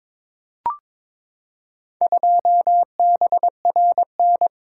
Morse code, heard as a keyed tone, about 22 wpm, 700 Hz. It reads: 2BRN
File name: practice / 22wpm / Four Letter-Number Combinations - 22wpm